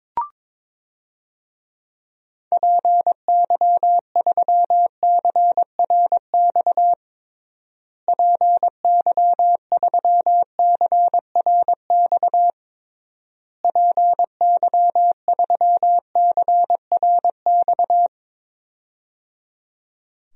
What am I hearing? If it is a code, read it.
Morse code, heard as a keyed tone, about 22 wpm, 700 Hz. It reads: PY3CRX PY3CRX PY3CRX
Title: DXCC-100 Call Signs - 6 Characters - Repeated 3x - 22wpm